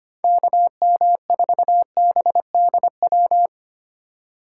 Morse code, heard as a keyed tone, about 31 wpm, 700 Hz. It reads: KM4BDW